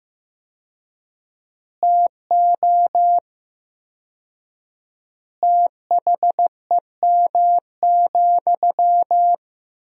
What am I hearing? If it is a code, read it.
Morse code, heard as a keyed tone, about 15 wpm, 700 Hz. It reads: TO THEM,
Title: 4x Word-Spacing - Sentences from Top 400 Words - 15wpm